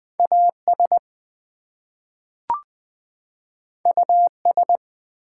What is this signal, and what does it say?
Morse code, heard as a keyed tone, about 20 wpm, 700 Hz. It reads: AS US
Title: Two Letter Words - 20wpm